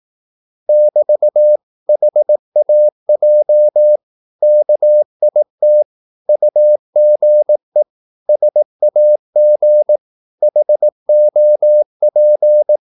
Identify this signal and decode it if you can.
Morse code, heard as a keyed tone, about 18 wpm, 600 Hz. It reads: <BT> HAJ KIT UGE SAG HOP
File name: Korte ord DK 18wpm